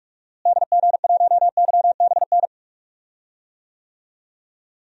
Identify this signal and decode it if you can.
Morse code, heard as a keyed tone, about 44 wpm, 700 Hz. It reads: DG1YBN
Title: DXCC-100 Call Signs - 6 Characters - Repeated 2x - 45wpm